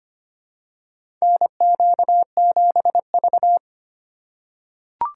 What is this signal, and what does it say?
Morse code, heard as a keyed tone, about 25 wpm, 700 Hz. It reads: NQ7V
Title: USA Call Signs - 4x Character Spacing - 25wpm